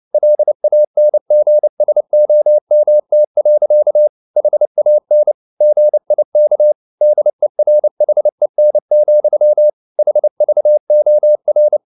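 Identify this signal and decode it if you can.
Morse code, heard as a keyed tone, about 29 wpm, 600 Hz. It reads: LANGSOMT. HAN GIK DERHEN, HVOR
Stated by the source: Fast (QRQ) Morse at 29 wpm, from CW med Gnister.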